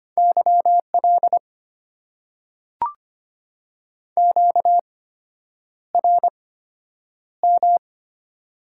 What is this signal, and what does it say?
Morse code, heard as a keyed tone, about 25 wpm, spaced as for 9 wpm, 700 Hz. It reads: YL QRM
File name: Top 90 QSO Elements - 8x Character Spacing - 25wpm